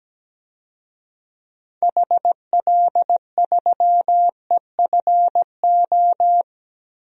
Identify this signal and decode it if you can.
Morse code, heard as a keyed tone, about 17 wpm, 700 Hz. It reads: HL3EFO
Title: DXCC-100 Call Signs - 6 Characters - 17wpm